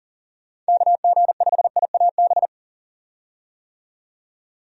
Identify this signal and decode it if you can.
Morse code, heard as a keyed tone, about 40 wpm, 700 Hz. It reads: KG5IAB
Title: USA Call Signs - 6 Characters - 40wpm